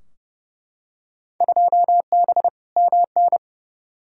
Morse code, heard as a keyed tone, about 30 wpm, 700 Hz. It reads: 2B MN
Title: Field Day Exchanges - 30wpm